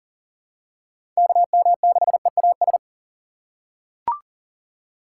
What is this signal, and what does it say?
Morse code, heard as a keyed tone, about 38 wpm, 700 Hz. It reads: KM6EAS
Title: USA Call Signs - 3x Character Spacing - 40wpm